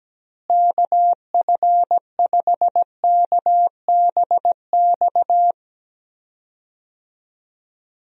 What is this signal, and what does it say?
Morse code, heard as a keyed tone, about 17 wpm, 700 Hz. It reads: KF5KBX
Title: USA Call Signs - 6 Characters - Repeated 2x - 17wpm